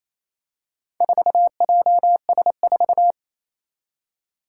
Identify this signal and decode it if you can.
Morse code, heard as a keyed tone, about 27 wpm, 700 Hz. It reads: 4JS4
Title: Four Challenging Letter-Number Combinations 28wpm